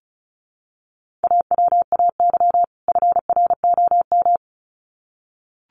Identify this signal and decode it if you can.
Morse code, heard as a keyed tone, about 34 wpm, 700 Hz. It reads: AWAY FROM
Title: Sets of 2 Words 35wpm